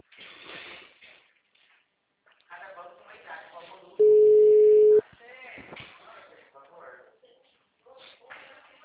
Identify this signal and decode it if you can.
Ringback tone (Europe, 425 Hz), with speech.